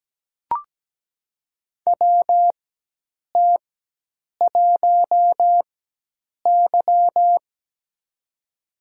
Morse code, heard as a keyed tone, about 17 wpm, spaced as for 8 wpm, 700 Hz. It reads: WT1Y
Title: USA Call Signs - 4 Characters - 4x Character Spacing - 17wpm